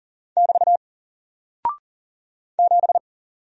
Morse code, heard as a keyed tone, about 40 wpm, 700 Hz. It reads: <BT> 7
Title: Single Character (Letter:Number:Punctuation:Prosign) 40wpm